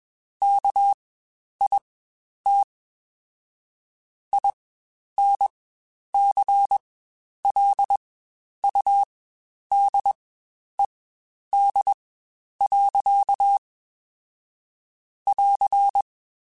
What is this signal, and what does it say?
Morse code, heard as a keyed tone, about 21 wpm, spaced as for 10 wpm, 800 Hz. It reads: KIT INCLUDED. <AR>
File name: Quotes Spaced / Quotes Spaced-mp3s / Airtight sp @20